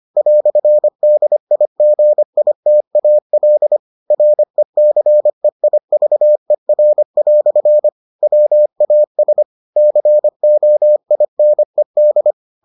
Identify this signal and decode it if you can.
Morse code, heard as a keyed tone, about 25 wpm, 600 Hz. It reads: "DIGITAL RECEIVER" WAS COINED